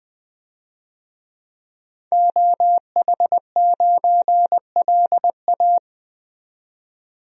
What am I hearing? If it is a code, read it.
Morse code, heard as a keyed tone, about 20 wpm, 700 Hz. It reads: OH9LA
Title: DXCC-100 Call Signs - 5 Characters - 20wpm